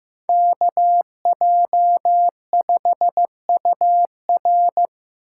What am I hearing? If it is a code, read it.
Morse code, heard as a keyed tone, about 18 wpm, 700 Hz. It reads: KJ5UR
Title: Speed Racing - USA 2x2 Call Signs 15wpm